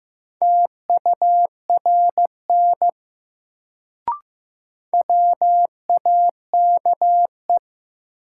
Morse code, heard as a keyed tone, about 15 wpm, 700 Hz. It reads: TURN WAKE
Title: Sets of 4 Words - New Word Drill - 15wpm